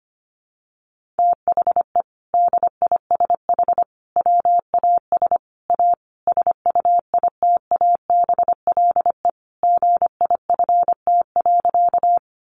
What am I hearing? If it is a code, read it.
Morse code, heard as a keyed tone, about 25 wpm, 700 Hz. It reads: THE DISH WAS A SUITABLE GIFT.
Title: Sentences from Top 1400 Words 25wpm